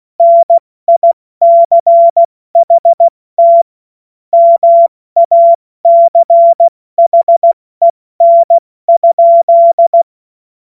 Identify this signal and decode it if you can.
Morse code, heard as a keyed tone, about 16 wpm, 680 Hz. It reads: NICHT MACHEN?
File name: Witz-29-Juli-2025-70-F80